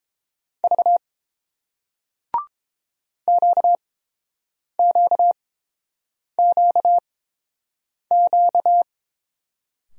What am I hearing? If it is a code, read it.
Morse code, heard as a keyed tone, about 30 wpm, spaced as for 7 wpm, 690 Hz. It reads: V QQQQ